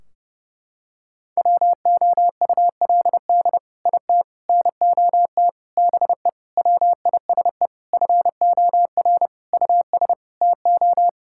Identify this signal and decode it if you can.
Morse code, heard as a keyed tone, about 30 wpm, 700 Hz. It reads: WOULD IT NOT BE WISE FOR US TO
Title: Sentences from Top 1600 Words 30wpm